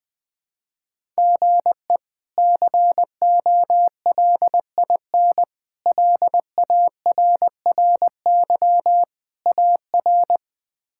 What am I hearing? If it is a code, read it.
Morse code, heard as a keyed tone, about 20 wpm, 700 Hz. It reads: GE COLIN LARRY AR
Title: K1USN SST QSO Practice - 20wpm